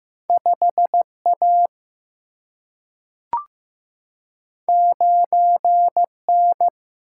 Morse code, heard as a keyed tone, about 15 wpm, 700 Hz. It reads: 5 A 9 N
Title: Single-Letter Single-Number 15wpm